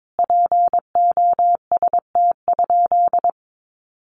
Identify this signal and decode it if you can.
Morse code, heard as a keyed tone, about 22 wpm, 700 Hz. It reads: POST?
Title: Sentences from Top 1000 Words 22wpm